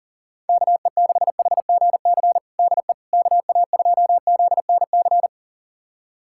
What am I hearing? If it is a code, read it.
Morse code, heard as a keyed tone, about 40 wpm, 700 Hz. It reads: KE6HGC DE KA2ZNC
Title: USA Callsign DE Callsign - 6 Characters - 40wpm